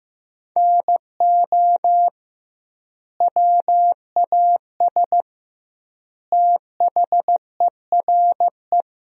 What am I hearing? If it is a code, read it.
Morse code, heard as a keyed tone, about 16 wpm, 700 Hz. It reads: NO WAS THERE
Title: 2x Word-Spacing - Sentences from Top 100 Words - 15wpm